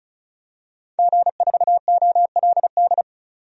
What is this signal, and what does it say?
Morse code, heard as a keyed tone, about 34 wpm, 700 Hz. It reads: G4OLD
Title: DXCC-100 Call Signs - 5 Characters - 35wpm